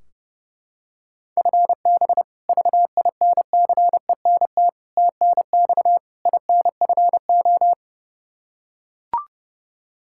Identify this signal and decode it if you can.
Morse code, heard as a keyed tone, about 32 wpm, 700 Hz. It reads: FB VINCENT TNX INFO
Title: Practice QSO - Exchange 3 - More Info - 30wpm